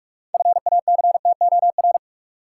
Morse code, heard as a keyed tone, about 44 wpm, 700 Hz. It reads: FACTOR
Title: Six Letter Words - 45wpm